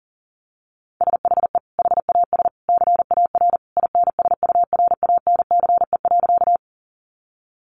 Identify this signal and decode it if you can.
Morse code, heard as a keyed tone, about 40 wpm, 700 Hz. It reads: SHE HAS CAR INSURANCE.
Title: Sentences from Top 1100 Words 40wpm